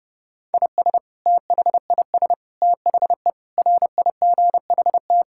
Morse code, heard as a keyed tone, about 32 wpm, 700 Hz. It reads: IS THIS THE RIGHT
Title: Sentences from Top 1000 Words - Review Part 6 - 30wpm